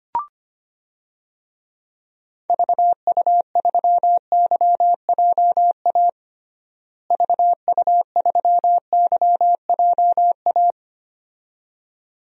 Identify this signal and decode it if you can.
Morse code, heard as a keyed tone, about 25 wpm, 700 Hz. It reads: VU3YJA VU3YJA
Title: DXCC-100 Call Signs - 6 Characters - Repeated 2x - 25wpm